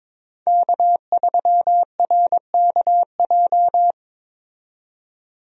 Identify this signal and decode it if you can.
Morse code, heard as a keyed tone, about 22 wpm, 700 Hz. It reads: K3RKJ
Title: USA Call Signs - 5 Characters - 22wpm